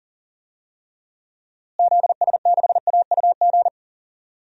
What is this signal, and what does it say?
Morse code, heard as a keyed tone, about 40 wpm, 700 Hz. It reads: ZS6AUG
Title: DXCC-100 Call Signs - 6 Characters - Repeated 2x - 40wpm